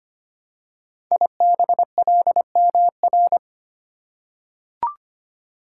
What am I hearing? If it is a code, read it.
Morse code, heard as a keyed tone, about 25 wpm, 700 Hz. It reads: IBLMR
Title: Code Groups - 5 Letters - 25wpm